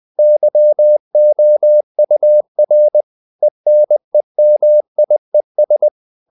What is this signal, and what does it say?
Morse code, heard as a keyed tone, about 20 wpm, 600 Hz. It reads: YOUR ENEMIES
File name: CW / Practice / EnglishSayings20